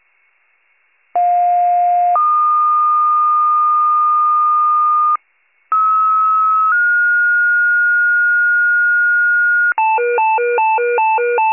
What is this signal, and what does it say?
Two-tone page, heard as a one-second tone, then a three-second tone, about 710 then 1220 Hz, about 1290 then 1430 Hz, with noise. It is heard from 1.2 to 9.7 s.